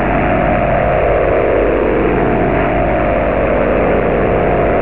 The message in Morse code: none

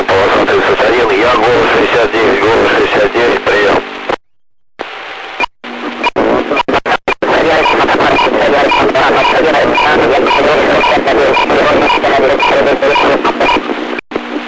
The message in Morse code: none

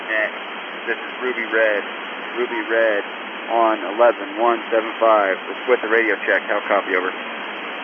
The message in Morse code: none

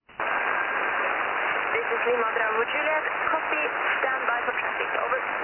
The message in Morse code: none

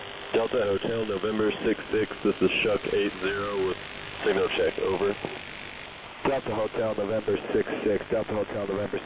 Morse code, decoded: none

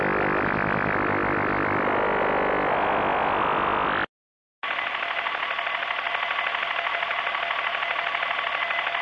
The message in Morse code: none

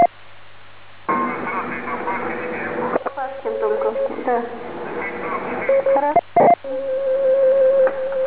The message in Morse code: none

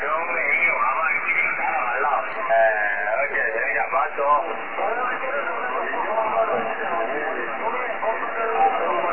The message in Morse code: none

